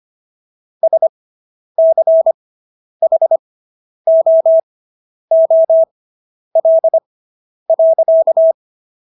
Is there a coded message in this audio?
Yes: Morse code from 0.8 to 8.5 s, heard as a keyed tone, about 25 wpm, spaced as for 10 wpm, 650 Hz. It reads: SCHOOL.